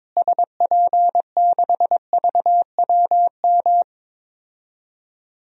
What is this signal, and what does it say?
Morse code, heard as a keyed tone, about 22 wpm, 700 Hz. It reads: SP6VWM
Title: DXCC-100 Call Signs - 6 Characters - Repeated 2x - 22wpm